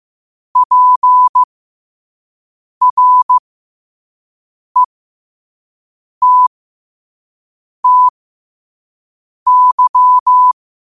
Morse code, heard as a keyed tone, about 15 wpm, spaced as for 5 wpm, 1000 Hz. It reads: PRETTY
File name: qotdfiles / 5wpm